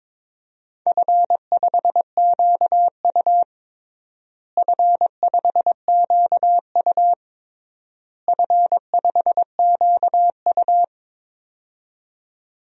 Morse code, heard as a keyed tone, about 22 wpm, 700 Hz. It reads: F5QU F5QU F5QU
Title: DXCC-100 Call Signs - 4 Characters - Repeated 3x - 22wpm